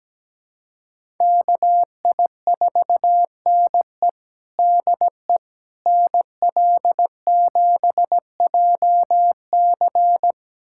Morse code, heard as a keyed tone, about 17 wpm, 700 Hz. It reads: KI4NE DE NL7JC
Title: USA Callsign DE Callsign 17wpm